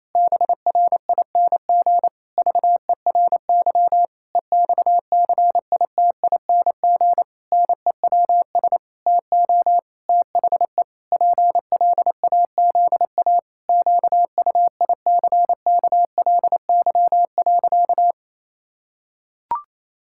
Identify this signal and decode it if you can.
Morse code, heard as a keyed tone, about 28 wpm, 700 Hz. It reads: BRING VERY EXCITING NEWS TO THE PLAZA QUICKLY.
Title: Pangram Challenge v2 - 28wpm